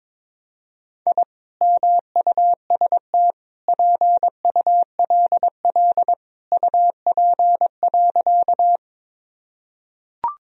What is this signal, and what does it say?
Morse code, heard as a keyed tone, about 22 wpm, 700 Hz. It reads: I MUST PULL UP.
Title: Sentences from Top 1000 Words - Review Part 2 - 22wpm